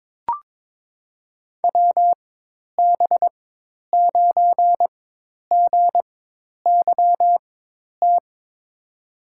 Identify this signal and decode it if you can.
Morse code, heard as a keyed tone, about 22 wpm, spaced as for 10 wpm, 700 Hz. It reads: WB9GYT
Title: USA Call Signs - 4x Character Spacing - 22wpm